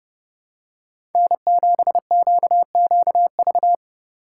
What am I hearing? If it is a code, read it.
Morse code, heard as a keyed tone, about 30 wpm, 700 Hz. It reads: N7QQV
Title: USA Call Signs - 5 Characters - 3x Character Spacing - 30wpm